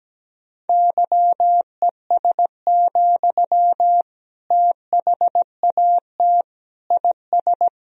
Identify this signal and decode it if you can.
Morse code, heard as a keyed tone, about 17 wpm, 700 Hz. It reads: YES, THAT IS
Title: Sentences from Top 500 Words and Abbreviations 17wpm